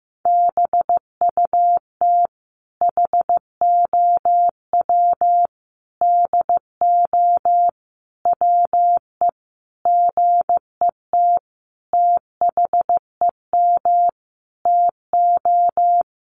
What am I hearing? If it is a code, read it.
Morse code, heard as a keyed tone, about 15 wpm, 700 Hz. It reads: BUT HOW DO WE GET THEM TO
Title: Sentences from Top 700 Words 15wpm